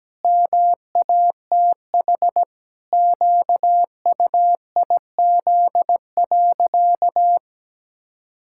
Morse code, heard as a keyed tone, about 17 wpm, 700 Hz. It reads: MATH QUIZ.
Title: Pangram Challenge v2 - 17wpm